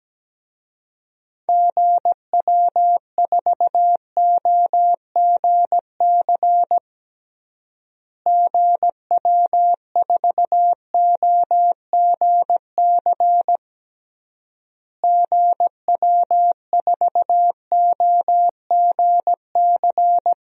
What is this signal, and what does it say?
Morse code, heard as a keyed tone, about 17 wpm, 700 Hz. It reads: GW4OGC GW4OGC GW4OGC
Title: DXCC-100 Call Signs - 6 Characters - Repeated 3x - 17wpm